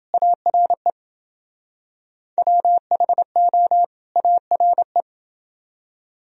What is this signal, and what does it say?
Morse code, heard as a keyed tone, about 28 wpm, 700 Hz. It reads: ARE WHO ARE